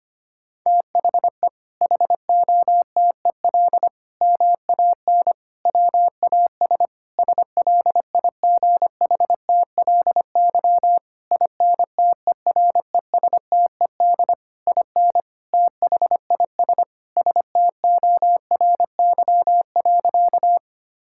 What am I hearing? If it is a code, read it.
Morse code, heard as a keyed tone, about 25 wpm, 700 Hz. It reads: THE HOTEL MAN WAS SLIGHTLY INTERESTED IN THIS STORY.
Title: Sentences from Top 1600 Words 25wpm